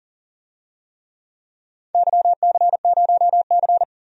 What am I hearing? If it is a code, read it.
Morse code, heard as a keyed tone, about 40 wpm, 700 Hz. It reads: YC0C
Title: DXCC-100 Call Signs - 4 Characters - 40wpm